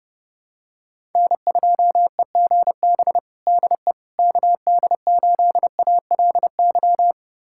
Morse code, heard as a keyed tone, about 30 wpm, 700 Hz. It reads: N2EGB DE KD8ALY
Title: USA Callsign DE Callsign 30wpm